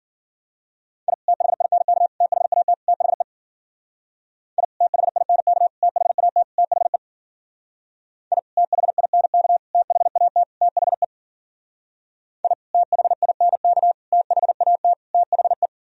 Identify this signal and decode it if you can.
Morse code, heard as a keyed tone, about 53 wpm, 700 Hz. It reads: I THINK THAT THE I THINK THAT THE I THINK THAT THE I THINK THAT THE